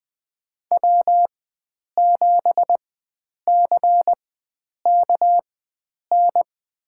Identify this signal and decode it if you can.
Morse code, heard as a keyed tone, about 21 wpm, spaced as for 9 wpm, 700 Hz. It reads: W7CKN
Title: USA Call Signs - 5 Characters - 4x Character Spacing - 20wpm